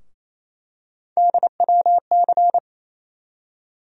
Morse code, heard as a keyed tone, about 28 wpm, 700 Hz. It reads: DWC
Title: USA Call Sign Postfixes - 3 Letters 28wpm